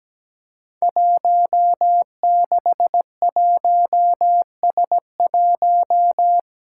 Morse code, heard as a keyed tone, about 17 wpm, 700 Hz. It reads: 161S1
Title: Five Challenging Letter-Number Combinations 17wpm